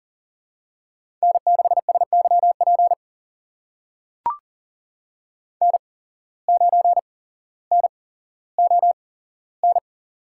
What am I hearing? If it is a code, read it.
Morse code, heard as a keyed tone, about 40 wpm, spaced as for 17 wpm, 700 Hz. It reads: N6SYP N9NON
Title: USA Call Signs - 5 Characters - 8x Character Spacing - 40wpm